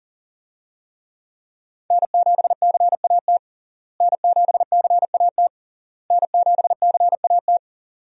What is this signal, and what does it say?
Morse code, heard as a keyed tone, about 37 wpm, 700 Hz. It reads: N7CAT N7CAT N7CAT
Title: USA Call Signs - 5 Characters - Repeated 3x - 40wpm